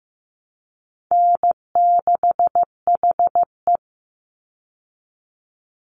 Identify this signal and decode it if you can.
Morse code, heard as a keyed tone, about 15 wpm, 700 Hz. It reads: N6HE
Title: Call Signs Repeated 3x - 15wpm